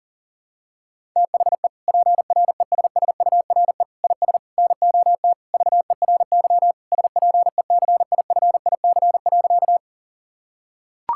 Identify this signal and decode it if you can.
Morse code, heard as a keyed tone, about 40 wpm, 700 Hz. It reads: THE PRESSURE IS NOT VERY SPECIFIC.
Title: Sentences from Top 1000 Words - Review Part 6 - 40wpm